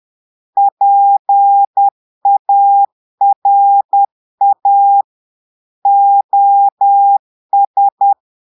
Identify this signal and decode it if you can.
Morse code, heard as a keyed tone, about 10 wpm, 800 Hz. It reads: PARA OS
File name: portugais texte en clair 2 vitesse 10 mots minutes